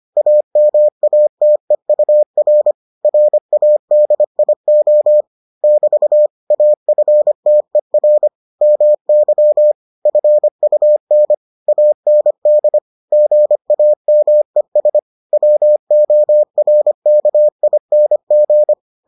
Morse code, heard as a keyed tone, about 25 wpm, 600 Hz. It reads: AMATEUR RADIO <BT> AFTER MY FUN AND GAMES WORKING